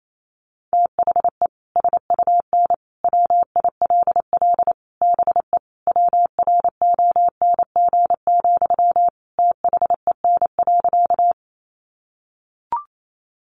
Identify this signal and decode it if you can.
Morse code, heard as a keyed tone, about 28 wpm, 700 Hz. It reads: THE SUN WILL BE WRONG, THEN.
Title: Sentences from Top 900 Words 28wpm